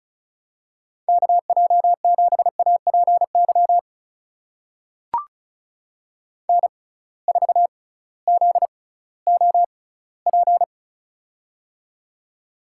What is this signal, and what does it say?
Morse code, heard as a keyed tone, about 36 wpm, spaced as for 19 wpm, 700 Hz. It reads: KJ7APY N4ZOP
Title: USA Call Signs - 6x Character Spacing - 35wpm